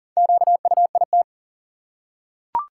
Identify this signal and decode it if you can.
Morse code, heard as a keyed tone, about 40 wpm, 700 Hz. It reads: QUIT
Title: Four Letter Words - 40wpm